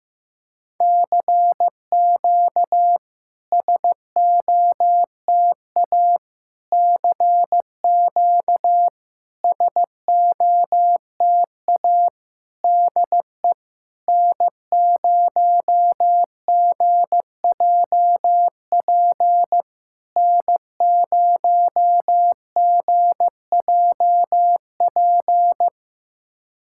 Morse code, heard as a keyed tone, about 15 wpm, 700 Hz. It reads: CQ SOTA CQ SOTA DE N0GJP N0GJP
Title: SOTA QSO Practice 15wpm